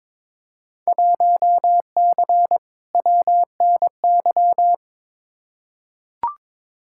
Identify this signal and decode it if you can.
Morse code, heard as a keyed tone, about 22 wpm, 700 Hz. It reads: 1C WNY
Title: Field Day Exchanges Encore - 22wpm